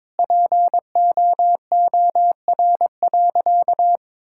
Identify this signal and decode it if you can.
Morse code, heard as a keyed tone, about 22 wpm, 700 Hz. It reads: POOR.
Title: Sentences from Top 1000 Words - Review Part 5 - 22wpm